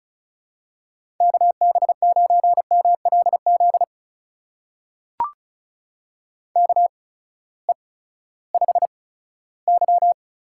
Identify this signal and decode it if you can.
Morse code, heard as a keyed tone, about 36 wpm, spaced as for 18 wpm, 700 Hz. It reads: KB9MLZ KE5Y
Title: USA Call Signs - 6 Characters - 8x Character Spacing - 35wpm